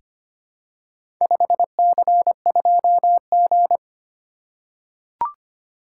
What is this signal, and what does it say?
Morse code, heard as a keyed tone, about 25 wpm, 700 Hz. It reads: 5C2G
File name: practice / 25wpm / Four Letter-Number Combinations - 25wpm